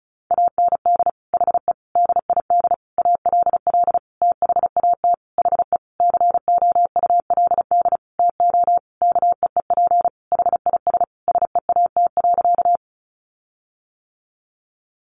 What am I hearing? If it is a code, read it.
Morse code, heard as a keyed tone, about 34 wpm, 700 Hz. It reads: AND HE DID ALL THAT HE COULD TO KEEP HIS SEAT.
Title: Sentences from Top 1500 Words 35wpm